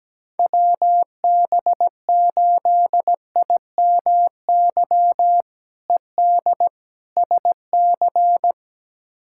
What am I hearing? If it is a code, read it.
Morse code, heard as a keyed tone, about 17 wpm, 700 Hz. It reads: WB8IMY ED SC
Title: K1USN SST QSO Practice - 17wpm